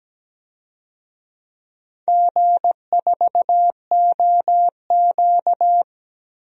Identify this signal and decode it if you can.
Morse code, heard as a keyed tone, about 17 wpm, 700 Hz. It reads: G4OQ
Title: DXCC-100 Call Signs - 4 Characters - 17wpm